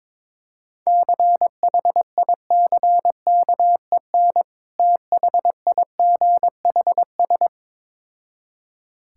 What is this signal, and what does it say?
Morse code, heard as a keyed tone, about 23 wpm, 690 Hz. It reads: CHICKEN THIGHS
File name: Difficult Words with Hints - 22wpm